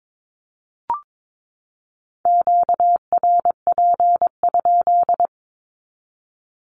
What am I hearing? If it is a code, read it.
Morse code, heard as a keyed tone, about 22 wpm, 700 Hz. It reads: QRP?
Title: Q-Codes 22wpm